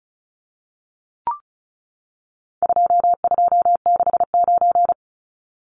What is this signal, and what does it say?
Morse code, heard as a keyed tone, about 34 wpm, 700 Hz. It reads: 2269
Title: Numbers - 4-Digit 35wpm